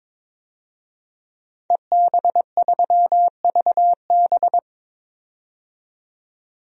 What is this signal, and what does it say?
Morse code, heard as a keyed tone, about 22 wpm, 700 Hz. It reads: EB3VB
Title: DXCC-100 Call Signs - 5 Characters - 22wpm